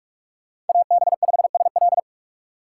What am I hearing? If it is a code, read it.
Morse code, heard as a keyed tone, about 41 wpm, 700 Hz. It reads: AB5SL
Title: USA Call Signs - 5 Characters - 6x Character Spacing - 45wpm